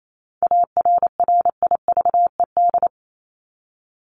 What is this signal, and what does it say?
Morse code, heard as a keyed tone, about 28 wpm, 700 Hz. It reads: ARRIVED
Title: ED Suffix - Top 100 Words with Base Words 28wpm